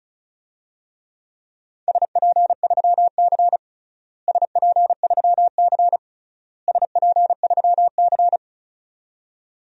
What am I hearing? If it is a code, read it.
Morse code, heard as a keyed tone, about 36 wpm, 690 Hz. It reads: SP3C SP3C SP3C
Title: DXCC-100 Call Signs - 4 Characters - Repeated 3x - 35wpm